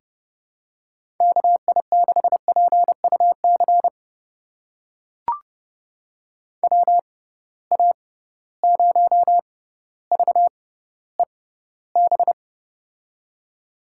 Morse code, heard as a keyed tone, about 30 wpm, spaced as for 15 wpm, 700 Hz. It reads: KI6PUC WA0VEB
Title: USA Call Signs - 6 Characters - 6x Character Spacing - 30wpm